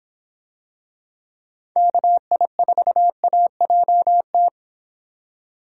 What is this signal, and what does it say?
Morse code, heard as a keyed tone, about 26 wpm, 700 Hz. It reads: KI4AJT